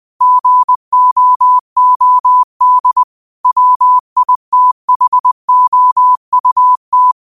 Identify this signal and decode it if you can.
Morse code, heard as a keyed tone, about 20 wpm, 1000 Hz. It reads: GOOD WITHOUT